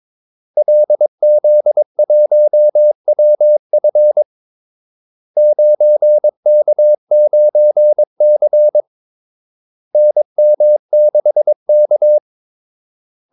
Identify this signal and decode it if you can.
Morse code, heard as a keyed tone, about 22 wpm, 600 Hz. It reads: LZ1WF 9K9C NM6K
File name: Callsigns-22wpm